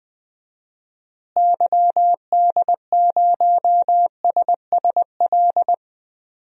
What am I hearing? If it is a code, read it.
Morse code, heard as a keyed tone, about 20 wpm, 700 Hz. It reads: YD0SSL
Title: DXCC-100 Call Signs - 6 Characters - Repeated 2x - 20wpm